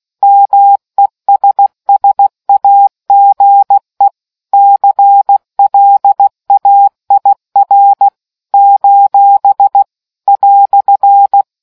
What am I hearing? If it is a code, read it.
Morse code, heard as a keyed tone, about 16 wpm, 800 Hz. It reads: MESSAGE CLAIR : "